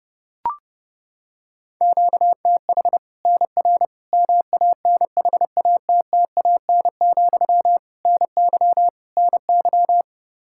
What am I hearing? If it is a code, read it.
Morse code, heard as a keyed tone, about 30 wpm, 700 Hz. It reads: QTH NR MANHATTAN, NY NY
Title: Practice QSO - Exchange 1 - Beginning - 30wpm